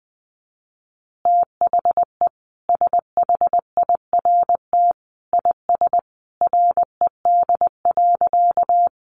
Morse code, heard as a keyed tone, about 19 wpm, 700 Hz. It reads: THE SHIRT IS RED.
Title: Additional Easy Phrases 20wpm